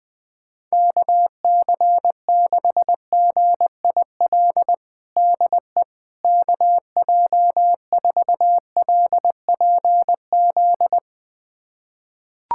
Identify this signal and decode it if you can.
Morse code, heard as a keyed tone, about 20 wpm, 700 Hz. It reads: KC6GIL DE KJ4LPZ
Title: USA Callsign DE Callsign - 6 Characters - 20wpm